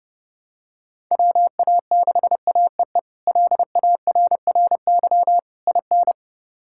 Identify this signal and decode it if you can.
Morse code, heard as a keyed tone, about 30 wpm, 700 Hz. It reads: WA6AEE LARRY IN
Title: K1USN SST QSO Practice - 30wpm